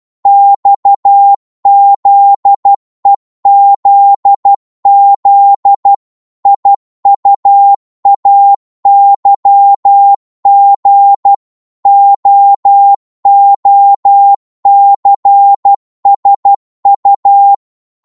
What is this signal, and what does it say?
Morse code, heard as a keyed tone, about 12 wpm, 800 Hz. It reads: XZEZZ IUAYG OOCSU
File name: Morse / ltr-12wpm